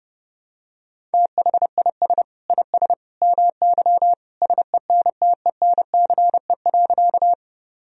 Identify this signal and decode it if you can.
Morse code, heard as a keyed tone, about 32 wpm, 700 Hz. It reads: THIS IS MY SENTENCE.
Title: Sentences from Top 1000 Words - Review Part 1 - 30wpm